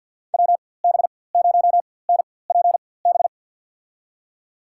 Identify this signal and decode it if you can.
Morse code, heard as a keyed tone, about 52 wpm, spaced as for 24 wpm, 700 Hz. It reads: WB0NPB
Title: USA Call Signs - 4x Character Spacing - 50wpm